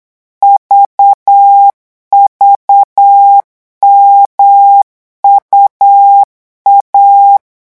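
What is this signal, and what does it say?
Morse code, heard as a keyed tone, about 8 wpm, 800 Hz. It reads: VVMUA
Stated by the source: Morse code revision lesson at 8 words per minute, from a French amateur radio course.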